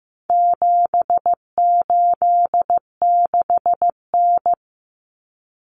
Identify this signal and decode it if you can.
Morse code, heard as a keyed tone, about 15 wpm, 700 Hz. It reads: 786N
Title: Common Cut Numbers - 4-Digit 15wpm